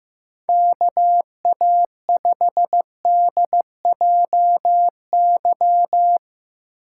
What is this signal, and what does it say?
Morse code, heard as a keyed tone, about 15 wpm, 700 Hz. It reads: KA5DJY
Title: USA Call Signs - 6 Characters - 6x Character Spacing - 15wpm